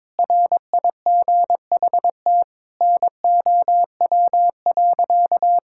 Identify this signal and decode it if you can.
Morse code, heard as a keyed tone, about 22 wpm, 700 Hz. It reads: RIGHT NOW.
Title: Sentences from Top 1600 Words 22wpm